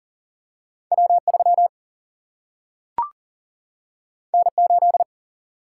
Morse code, heard as a keyed tone, about 42 wpm, 700 Hz. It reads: W 3 N 8
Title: Single-Letter Single-Number 40wpm